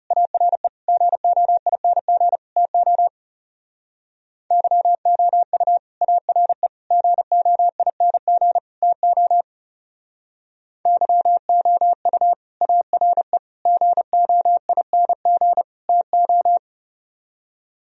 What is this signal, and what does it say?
Morse code, heard as a keyed tone, about 37 wpm, 700 Hz. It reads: ARE GOING TO YOU ARE GOING TO YOU ARE GOING TO
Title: Speed-Racing - Sets of 4 Words 30wpm